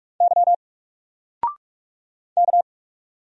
Morse code, heard as a keyed tone, about 44 wpm, 700 Hz. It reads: Y K
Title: Single Letter-Number 45wpm